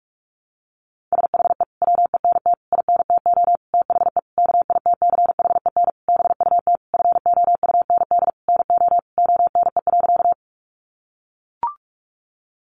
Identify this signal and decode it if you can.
Morse code, heard as a keyed tone, about 44 wpm, 700 Hz. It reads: SHE WENT INTO THE KITCHEN BUT FOUND NO ONE.
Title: Sentences from Top 1400 Words 45wpm